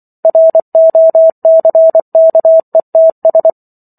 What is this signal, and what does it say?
Morse code, heard as a keyed tone, about 24 wpm, 650 Hz. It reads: ROCKETS